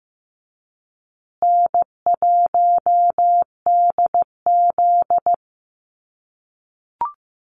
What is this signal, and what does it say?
Morse code, heard as a keyed tone, about 15 wpm, 700 Hz. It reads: N1DZ
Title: USA Call Signs Repeated 2x 15wpm